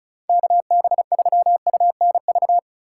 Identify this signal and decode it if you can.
Morse code, heard as a keyed tone, about 36 wpm, 700 Hz. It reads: KB3UNV
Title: USA Call Signs - 6 Characters - 2x Character Spacing - 35wpm